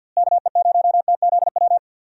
Morse code, heard as a keyed tone, about 51 wpm, 700 Hz. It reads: KE0TZW
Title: USA Call Signs - 6 Characters - Repeated 3x - 50wpm